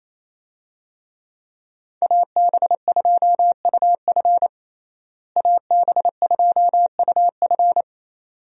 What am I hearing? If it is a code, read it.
Morse code, heard as a keyed tone, about 28 wpm, 700 Hz. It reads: AB2UF AB2UF
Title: USA Call Signs - 5 Characters - Repeated 2x - 28wpm